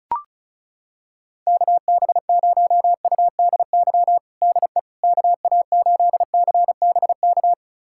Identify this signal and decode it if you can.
Morse code, heard as a keyed tone, about 34 wpm, 700 Hz. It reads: KB0UDY DE KA8CBK
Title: USA Callsign DE Callsign - 6 Characters - 35wpm